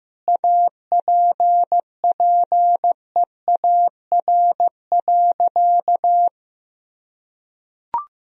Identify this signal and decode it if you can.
Morse code, heard as a keyed tone, about 15 wpm, 700 Hz. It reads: APPEAR.
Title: Sentences from Top 1000 Words - Review Part 2 - 15wpm